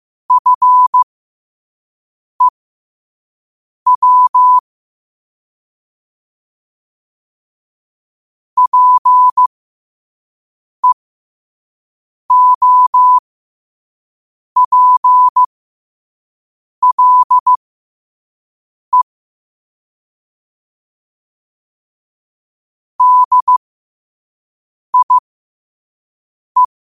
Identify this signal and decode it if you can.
Morse code, heard as a keyed tone, about 15 wpm, spaced as for 5 wpm, 1000 Hz. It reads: FEW PEOPLE DIE
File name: qotdfiles / 5wpm